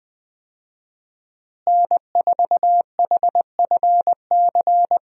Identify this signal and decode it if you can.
Morse code, heard as a keyed tone, about 20 wpm, 700 Hz. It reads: N4HFC
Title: USA Call Signs - 5 Characters - Repeated 3x - 20wpm